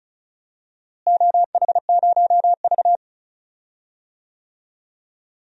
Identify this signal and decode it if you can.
Morse code, heard as a keyed tone, about 34 wpm, 700 Hz. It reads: OH0V
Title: DXCC-100 Call Signs - 4 Characters - Repeated 2x - 35wpm